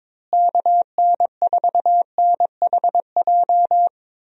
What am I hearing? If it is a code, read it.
Morse code, heard as a keyed tone, about 22 wpm, 700 Hz. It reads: KN4NHJ
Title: USA Call Signs - 6 Characters - Repeated 2x - 22wpm